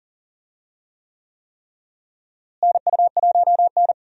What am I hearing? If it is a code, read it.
Morse code, heard as a keyed tone, about 38 wpm, 700 Hz. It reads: NU1N
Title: USA Call Signs - 4 Characters - Repeated 3x - 40wpm